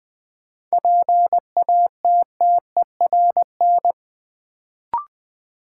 Morse code, heard as a keyed tone, about 20 wpm, 700 Hz. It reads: PATTERN
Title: Sentences from Top 1000 Words - New Word Drill - 20wpm